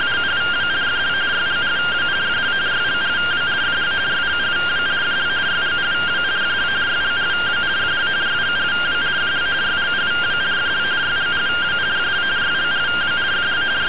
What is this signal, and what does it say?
Two-tone teleprinter signal, about 170 Hz shift.